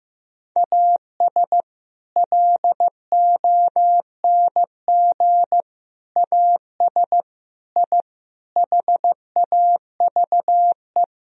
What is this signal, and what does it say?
Morse code, heard as a keyed tone, about 15 wpm, 700 Hz. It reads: AS LONG AS I HAVE
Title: Sentences from Top 1000 Words - Review Part 3 - 15wpm